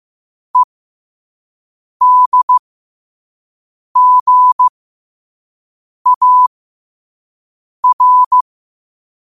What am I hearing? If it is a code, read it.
Morse code, heard as a keyed tone, about 15 wpm, spaced as for 5 wpm, 1000 Hz. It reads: EDGAR
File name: 5wpm